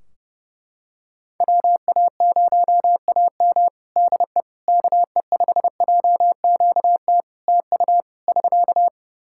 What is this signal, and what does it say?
Morse code, heard as a keyed tone, about 29 wpm, 700 Hz. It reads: WA0AM DE KE5JQT TU <SK>
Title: Practice QSO - Exchange 4 - Ending - 30wpm